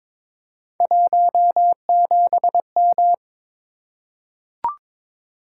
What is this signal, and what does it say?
Morse code, heard as a keyed tone, about 22 wpm, 700 Hz. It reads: 17M
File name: practice / 22wpm / Three Letter-Number Combinations - 22wpm